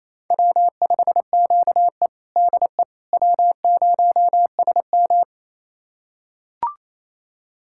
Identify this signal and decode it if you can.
Morse code, heard as a keyed tone, about 28 wpm, 700 Hz. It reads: W5QE DE W0SM
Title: USA Callsign DE Callsign - 4 Characters - 28wpm